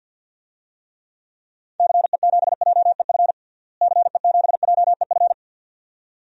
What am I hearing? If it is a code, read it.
Morse code, heard as a keyed tone, about 52 wpm, 700 Hz. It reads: CE7JEF CE7JEF
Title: DXCC-100 Call Signs - 6 Characters - Repeated 2x - 50wpm